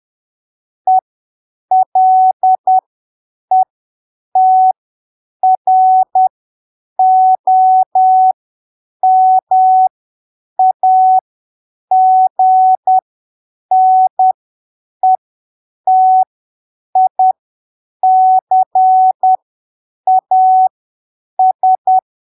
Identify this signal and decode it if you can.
Morse code, heard as a keyed tone, about 10 wpm, spaced as for 7 wpm, 750 Hz. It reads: ELETROMAGNETICAS
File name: arquivos / texto / texto 13 10ppm